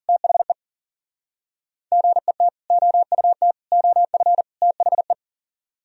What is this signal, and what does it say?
Morse code, heard as a keyed tone, about 51 wpm, 700 Hz. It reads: THE GET OUT OF THE